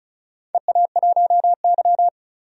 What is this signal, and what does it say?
Morse code, heard as a keyed tone, about 36 wpm, 700 Hz. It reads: EA1Y
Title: DXCC-100 Call Signs - 4 Characters - 35wpm